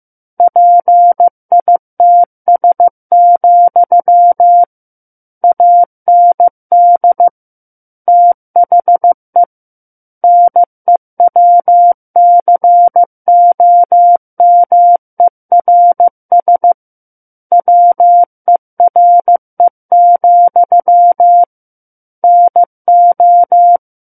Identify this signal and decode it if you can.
Morse code, heard as a keyed tone, about 15 wpm, 700 Hz. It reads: PITS, AND THE NEWCOMERS WERE, NO
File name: Book One / 06-Chapter 6 - 15 WPM